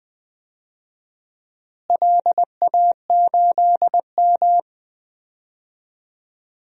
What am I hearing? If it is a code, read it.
Morse code, heard as a keyed tone, about 20 wpm, 700 Hz. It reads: LA8M
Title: DXCC-100 Call Signs - 4 Characters - Repeated 2x - 20wpm